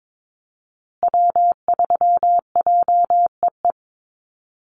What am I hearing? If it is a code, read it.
Morse code, heard as a keyed tone, about 22 wpm, 700 Hz. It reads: W3JEE
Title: USA Call Signs Repeated 2x 22wpm